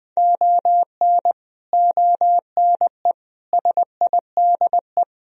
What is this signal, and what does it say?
Morse code, heard as a keyed tone, about 20 wpm, 700 Hz. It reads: ON ONE SIDE
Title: Speed-Racing - Sets of 5 Words 17wpm